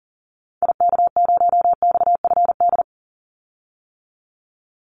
Morse code, heard as a keyed tone, about 40 wpm, 700 Hz. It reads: IK0XFD
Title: Call Signs - 40wpm